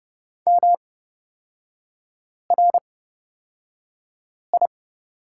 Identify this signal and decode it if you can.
Morse code, heard as a keyed tone, about 30 wpm, spaced as for 5 wpm, 700 Hz. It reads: MRI
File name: Lesson 38e - Letter q - Three Characters 30wpm